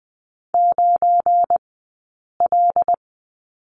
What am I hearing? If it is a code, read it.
Morse code, heard as a keyed tone, about 20 wpm, spaced as for 8 wpm, 700 Hz. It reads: 9L
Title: 183 Lesson 30d - Letter b - Two Characters 20wpm